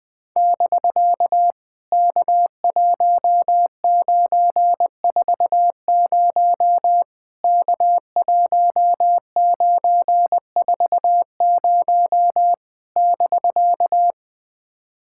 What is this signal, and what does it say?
Morse code, heard as a keyed tone, about 20 wpm, 700 Hz. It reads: <BK> K1940 K1940 <BK>
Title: POTA QSO Practice - 20wpm